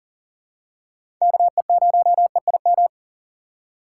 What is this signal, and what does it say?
Morse code, heard as a keyed tone, about 37 wpm, 700 Hz. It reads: KE0EIM
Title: USA Call Signs - 2x Character Spacing - 40wpm